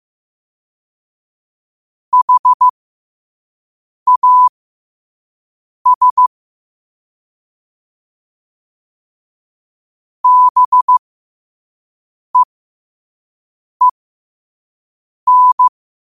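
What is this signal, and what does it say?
Morse code, heard as a keyed tone, about 15 wpm, spaced as for 5 wpm, 1000 Hz. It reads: HAS BEEN
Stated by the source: Morse code quote of the day sent at 5 wpm.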